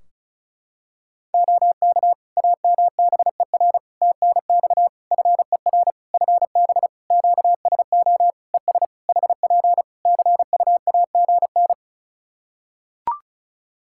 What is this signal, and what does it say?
Morse code, heard as a keyed tone, about 36 wpm, 700 Hz. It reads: OK AMBER TNX FER FB QSO ES HP CUAGN
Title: Practice QSO - Exchange 4 - Ending - 35wpm